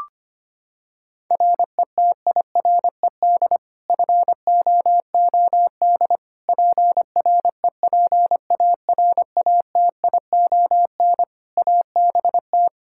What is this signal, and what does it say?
Morse code, heard as a keyed tone, about 25 wpm, 700 Hz. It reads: RETIRED FOOD PREPARATION ABT
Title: K1USN SST QSO Practice 25wpm